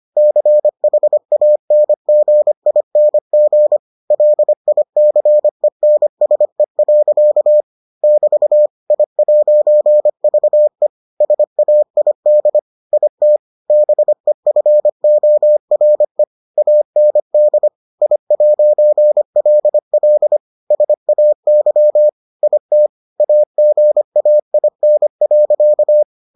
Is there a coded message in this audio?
Yes: Morse code from 0.2 to 26.0 s, heard as a keyed tone, about 25 wpm, 600 Hz. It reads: CHANGING LICENSE. <BT> I'VE SAID IT BEFORE AND I'LL SAY IT AGAIN.